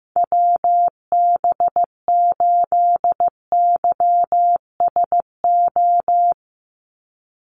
Morse code, heard as a keyed tone, about 15 wpm, 700 Hz. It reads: WB8YSO
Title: USA Call Signs 15wpm